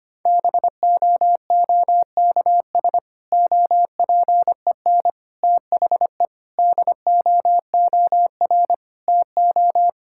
Morse code, heard as a keyed tone, about 25 wpm, 700 Hz. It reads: BOOKS OPEN THE DOOR TO
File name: practice / 25wpm / Difficult Sentences with Hints - 25wpm